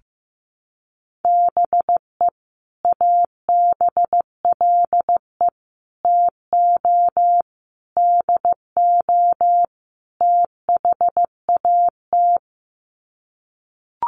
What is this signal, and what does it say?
Morse code, heard as a keyed tone, about 15 wpm, 700 Hz. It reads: BE ABLE TO DO THAT
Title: Sets of 5 Words Encore2 15wpm